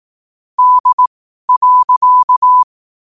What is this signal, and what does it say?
Morse code, heard as a keyed tone, about 18 wpm, spaced as for 12 wpm, 1000 Hz. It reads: D.
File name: qotdfiles / 12wpm